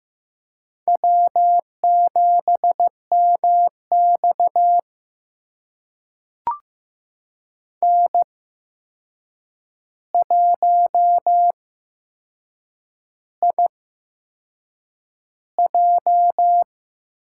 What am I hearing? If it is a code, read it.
Morse code, heard as a keyed tone, about 15 wpm, spaced as for 6 wpm, 700 Hz. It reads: W7MX N1IJ
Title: USA Call Signs - 4 Characters - 8x Character Spacing - 15wpm